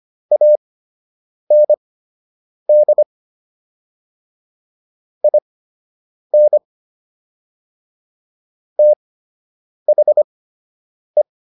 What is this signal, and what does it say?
Morse code, heard as a keyed tone, about 25 wpm, spaced as for 8 wpm, 600 Hz. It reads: AND IN THE